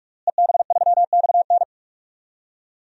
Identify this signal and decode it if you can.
Morse code, heard as a keyed tone, about 42 wpm, 700 Hz. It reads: EB3XN
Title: DXCC-100 Call Signs - 5 Characters - Repeated 3x - 45wpm